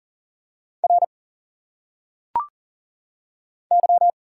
Morse code, heard as a keyed tone, about 40 wpm, 700 Hz. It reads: R Y
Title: Single Letters - Phonetic Pronunciation 40wpm